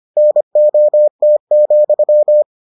Morse code, heard as a keyed tone, about 25 wpm, 600 Hz. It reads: NOT,